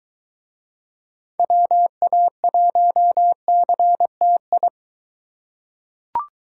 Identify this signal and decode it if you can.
Morse code, heard as a keyed tone, about 21 wpm, 700 Hz. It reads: WA1CTI